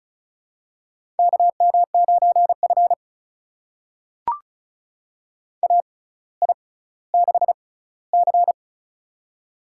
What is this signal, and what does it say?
Morse code, heard as a keyed tone, about 36 wpm, spaced as for 18 wpm, 700 Hz. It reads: KM9F AI6C
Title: USA Call Signs - 4 Characters - 6x Character Spacing - 35wpm